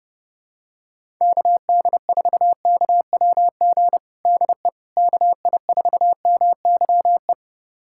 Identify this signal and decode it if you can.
Morse code, heard as a keyed tone, about 32 wpm, 700 Hz. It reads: KD4KWG DE KI4MYE
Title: USA Callsign DE Callsign 30wpm